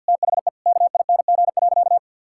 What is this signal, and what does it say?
Morse code, heard as a keyed tone, about 51 wpm, 700 Hz. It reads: THE KING.
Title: Sentences from Top 1000 Words - Review Part 6 - 50wpm